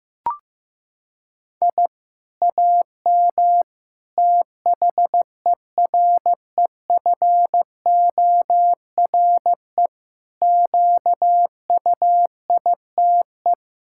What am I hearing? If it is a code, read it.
Morse code, heard as a keyed tone, about 15 wpm, 700 Hz. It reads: I AM THEREFORE QUITE
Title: Sentences from Top 1000 Words - Review Part 6 - 15wpm